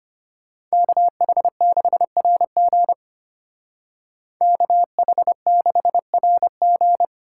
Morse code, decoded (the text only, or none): KH6RG KH6RG